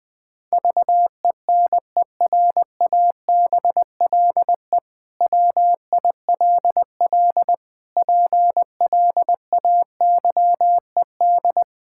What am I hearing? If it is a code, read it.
Morse code, heard as a keyed tone, about 20 wpm, 700 Hz. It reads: VENERABLE WILL PLAYED
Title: Pangram Challenge v2 - 20wpm